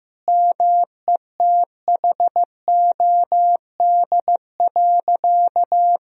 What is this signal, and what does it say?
Morse code, heard as a keyed tone, about 15 wpm, 700 Hz. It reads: METHOD.
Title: Sentences from Top 1000 Words - Review Part 3 - 15wpm